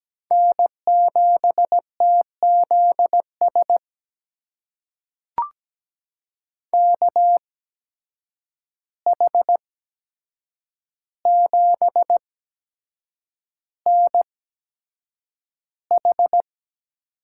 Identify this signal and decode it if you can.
Morse code, heard as a keyed tone, about 17 wpm, spaced as for 7 wpm, 700 Hz. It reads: N7TZS KH7NH
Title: USA Call Signs - 5 Characters - 8x Character Spacing - 17wpm